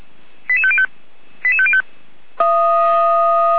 Five-tone selective call (ZVEI), heard as quick runs of five tones, tones of about 70 ms. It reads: 88505 88505
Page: /455/SDR/ZVEI/88191/